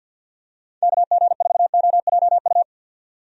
Morse code, heard as a keyed tone, about 51 wpm, 700 Hz. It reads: KG4OJU
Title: USA Call Signs - 6 Characters - 50wpm